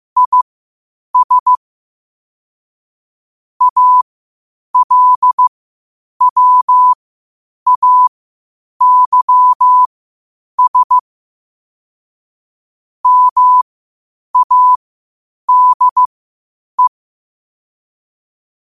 Morse code, heard as a keyed tone, about 15 wpm, spaced as for 8 wpm, 1000 Hz. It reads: IS ALWAYS MADE
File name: qotdfiles / 8wpm